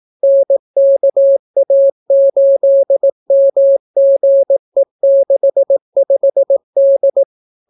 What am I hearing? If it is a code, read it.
Morse code, heard as a keyed tone, about 18 wpm, 550 Hz. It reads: NKA8MGE65D